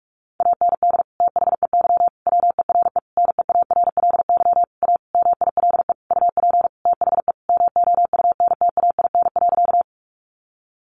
Morse code, heard as a keyed tone, about 44 wpm, 700 Hz. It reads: AND THEY WERE NEARLY A MILE UP THE MOUNTAIN.
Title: Sentences from Top 1300 Words 45wpm